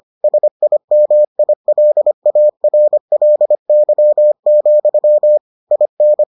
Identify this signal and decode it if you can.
Morse code, heard as a keyed tone, about 25 wpm, 600 Hz. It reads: SIMILARLY, IN